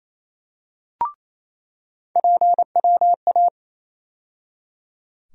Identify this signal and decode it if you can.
Morse code, heard as a keyed tone, about 28 wpm, 700 Hz. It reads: PWA
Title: USA Call Sign Postfixes - 3 Letters 28wpm